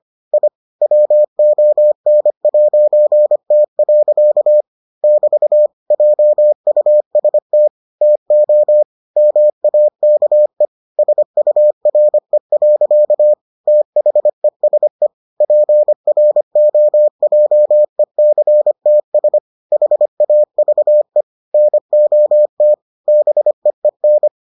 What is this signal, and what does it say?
Morse code, heard as a keyed tone, about 25 wpm, 600 Hz. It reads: I WON'T. <BT> JUST TO MAKE SURE. THESE PROJECTS HAVE NOT BEEN